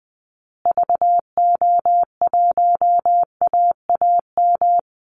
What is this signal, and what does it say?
Morse code, heard as a keyed tone, about 20 wpm, 700 Hz. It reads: VO1AAM
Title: Call Signs Repeated 2x - 20wpm